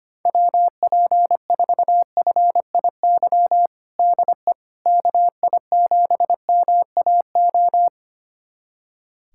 Morse code, heard as a keyed tone, about 25 wpm, 700 Hz. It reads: WP4FIY DE KI7MAO
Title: USA Callsign DE Callsign 25wpm